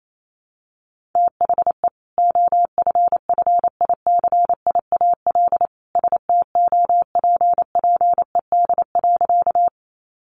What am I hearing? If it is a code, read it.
Morse code, heard as a keyed tone, about 28 wpm, 700 Hz. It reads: THE OFFICIAL STOPPED.
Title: A Sound Of Thunder 28wpm